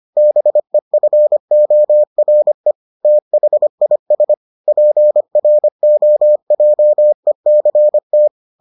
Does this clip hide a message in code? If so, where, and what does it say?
Yes: Morse code from 0.2 to 8.3 s, heard as a keyed tone, about 25 wpm, 600 Hz. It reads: BEFORE THIS PROJECT